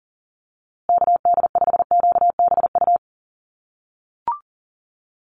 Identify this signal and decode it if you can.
Morse code, heard as a keyed tone, about 40 wpm, 700 Hz. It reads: KD5QBU
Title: USA Call Signs Repeated 3x 40wpm